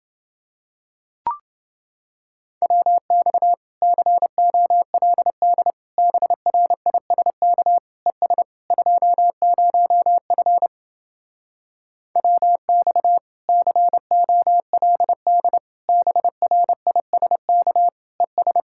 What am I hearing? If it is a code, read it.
Morse code, heard as a keyed tone, about 30 wpm, 700 Hz. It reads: WX COLD BRISK ES 20F WX COLD BRISK ES